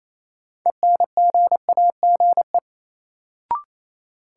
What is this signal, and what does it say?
Morse code, heard as a keyed tone, about 27 wpm, 700 Hz. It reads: ENGAGE
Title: Sentences from Top 1500 Words - New Word Drill Alt - 28wpm